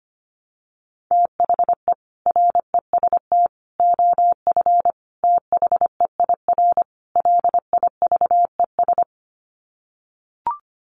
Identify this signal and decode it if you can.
Morse code, heard as a keyed tone, about 25 wpm, 700 Hz. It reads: THE REST OF THEIR LIVES
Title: Sets of 5 Words Encore 25wpm